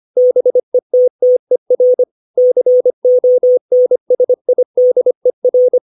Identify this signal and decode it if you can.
Morse code, heard as a keyed tone, about 25 wpm, 500 Hz. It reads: BETTER CONSIDER
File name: 632 words / 632 Words 25WPM